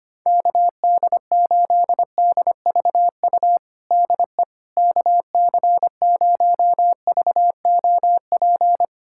Morse code, heard as a keyed tone, about 25 wpm, 700 Hz. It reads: KD8DVU DE KC0VOP
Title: USA Callsign DE Callsign 25wpm